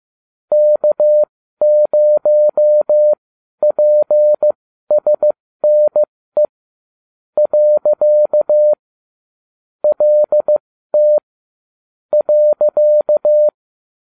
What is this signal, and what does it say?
Morse code, heard as a keyed tone, about 15 wpm, spaced as for 12 wpm, 600 Hz. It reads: K0PSNE . LT .